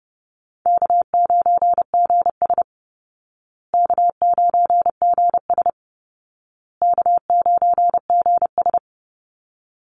Morse code, heard as a keyed tone, about 30 wpm, 700 Hz. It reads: K9GS K9GS K9GS
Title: Easy Call Signs Repeated 3x - 30wpm